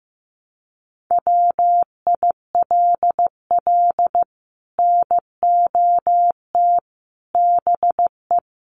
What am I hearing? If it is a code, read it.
Morse code, heard as a keyed tone, about 15 wpm, 700 Hz. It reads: WILL NOT BE
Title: Sets of 5 Words 15wpm